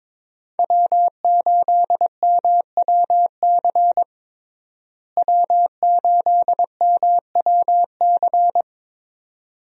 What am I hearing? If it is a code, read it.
Morse code, heard as a keyed tone, about 22 wpm, 700 Hz. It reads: W8MWC W8MWC
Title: USA Call Signs - 5 Characters - Repeated 2x - 22wpm